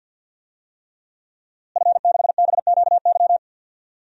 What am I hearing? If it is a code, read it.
Morse code, heard as a keyed tone, about 51 wpm, 700 Hz. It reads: F6BQY
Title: DXCC-100 Call Signs - 5 Characters - Repeated 3x - 50wpm